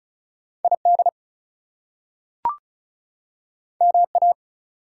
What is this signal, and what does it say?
Morse code, heard as a keyed tone, about 36 wpm, 700 Hz. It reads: ID MA
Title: US States and Canadian Provinces 35wpm